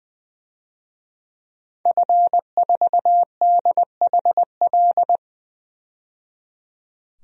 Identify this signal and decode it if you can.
Morse code, heard as a keyed tone, about 20 wpm, 700 Hz. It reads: F4DHL
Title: DXCC-100 Call Signs - 5 Characters - 20wpm